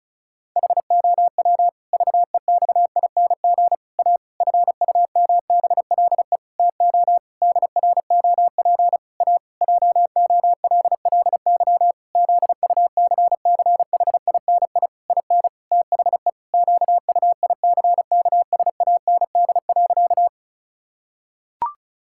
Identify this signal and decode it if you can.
Morse code, heard as a keyed tone, about 36 wpm, 700 Hz. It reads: HOW VEXING A FUMBLE TO DROP A JOLLY ZUCCHINI IN THE QUICKSAND.
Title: Pangram Challenge v2 - 35wpm